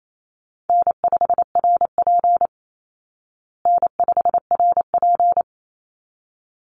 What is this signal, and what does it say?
Morse code, heard as a keyed tone, about 28 wpm, 700 Hz. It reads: N5RP N5RP
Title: Easy Call Signs Repeated 2x - 28wpm